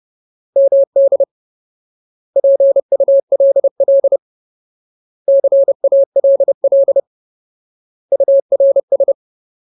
Morse code, heard as a keyed tone, about 30 wpm, 550 Hz. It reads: MD PULL CALL URS
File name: Ultimate Word List / 30WPM Ultimate